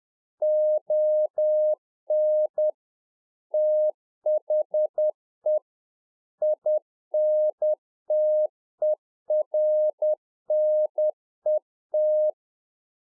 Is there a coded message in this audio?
Yes: Morse code from 0.4 to 12.3 s, heard as a keyed tone, about 10 wpm, 610 Hz. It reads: ON THE INTERNET